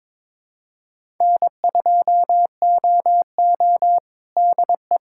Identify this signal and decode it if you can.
Morse code, heard as a keyed tone, about 22 wpm, 700 Hz. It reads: N2OO DE
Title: K1USN SST QSO Practice 22wpm